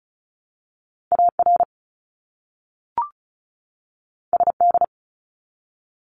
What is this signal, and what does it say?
Morse code, heard as a keyed tone, about 34 wpm, 700 Hz. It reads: AR SD
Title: ARRL Sections 35wpm